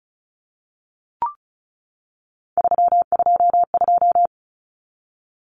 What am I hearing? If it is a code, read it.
Morse code, heard as a keyed tone, about 34 wpm, 700 Hz. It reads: 322
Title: Numbers - 3-Digit 35wpm